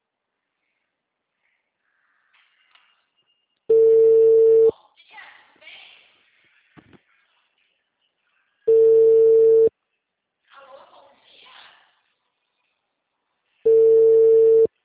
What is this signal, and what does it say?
Ringback tone (China, 450 Hz), with speech.
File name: out- -2002- - - .8752